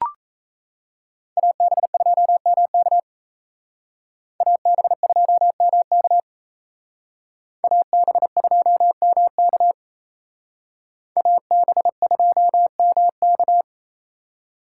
Morse code, heard as a keyed tone, about 35 wpm, 700 Hz. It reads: AB2MK AB2MK AB2MK AB2MK